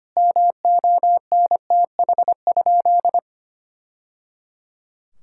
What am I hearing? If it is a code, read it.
Morse code, heard as a keyed tone, about 25 wpm, 700 Hz. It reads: MONTH?
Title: Sentences from Top 1000 Words - Review Part 5 - 25wpm